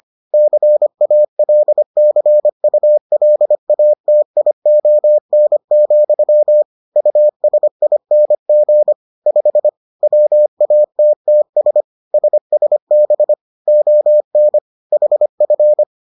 Morse code, heard as a keyed tone, about 25 wpm, 600 Hz. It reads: CALCULATION, USING 5 WATTS SSB ON HF